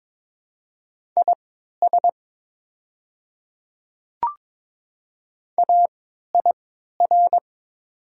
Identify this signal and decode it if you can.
Morse code, heard as a keyed tone, about 22 wpm, spaced as for 12 wpm, 700 Hz. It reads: IS AIR
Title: 3x Character Spacing - Top 200 Words - 22wpm